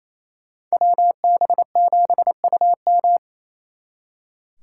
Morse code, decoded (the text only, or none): WB7UM